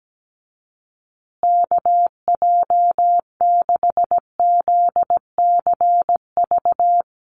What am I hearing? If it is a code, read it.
Morse code, heard as a keyed tone, about 17 wpm, 700 Hz. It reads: KJ6ZCV
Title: USA Call Signs Repeated 3x 17wpm